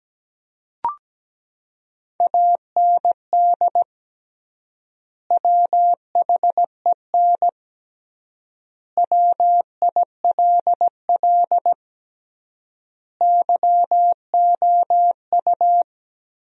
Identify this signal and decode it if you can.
Morse code, heard as a keyed tone, about 17 wpm, 700 Hz. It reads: AND WHEN WILL YOU
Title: 3x Word-Spacing - Sentences from Top 100 Words - 17wpm